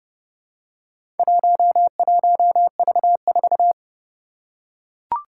Morse code, heard as a keyed tone, about 30 wpm, 700 Hz. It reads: 11V4
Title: Four Challenging Letter-Number Combinations 30wpm